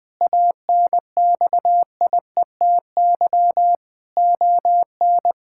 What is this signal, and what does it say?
Morse code, heard as a keyed tone, about 20 wpm, 700 Hz. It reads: ANXIETY ON
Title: Sentences from Top 1600 Words 20wpm